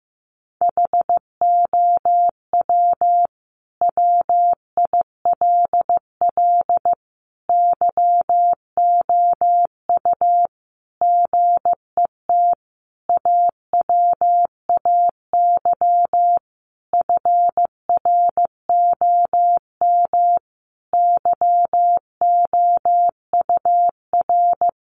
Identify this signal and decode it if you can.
Morse code, heard as a keyed tone, about 15 wpm, 700 Hz. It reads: HOW WILL YOU GET AWAY FROM YOUR
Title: Sentences from Top 1500 Words 15wpm